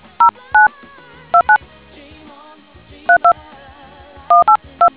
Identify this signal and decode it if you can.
DTMF, heard as telephone touch tones, tones of about 100 ms. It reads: 09 29 32 182